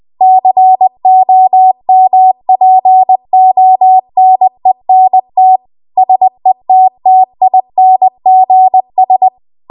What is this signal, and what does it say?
Morse code, heard as a keyed tone, about 20 wpm, 750 Hz. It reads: COMPONENT SETTINGS
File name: Archive / 20 WPM / 20WPM